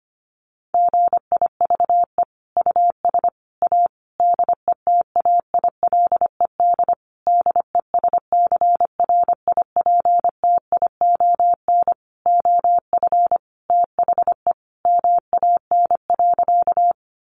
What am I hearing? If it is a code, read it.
Morse code, heard as a keyed tone, about 25 wpm, 700 Hz. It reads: GIVE US A DETAILED DESCRIPTION OF THE MAN.
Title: Sentences from Top 1500 Words 25wpm